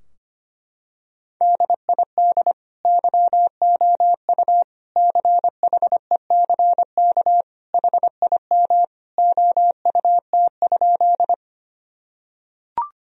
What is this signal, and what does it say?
Morse code, heard as a keyed tone, about 25 wpm, 700 Hz. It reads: DID YOU CHECK HIM OUT?
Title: Sentences from Top 1000 Words - Review Part 3 - 25wpm